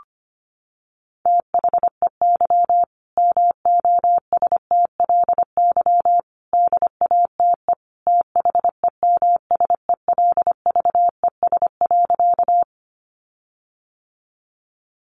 Morse code, heard as a keyed tone, about 25 wpm, 700 Hz. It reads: THEY MOSTLY DATE THEMSELVES.
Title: Sentences from Top 1300 Words 25wpm